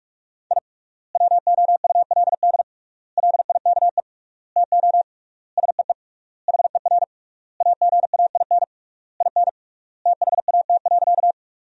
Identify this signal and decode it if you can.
Morse code, heard as a keyed tone, about 41 wpm, 700 Hz. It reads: I WOULD LIKE TO SEE HER AGAIN IN THAT.
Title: 3x Word-Spacing - Sentences from Top 200 Words - 45wpm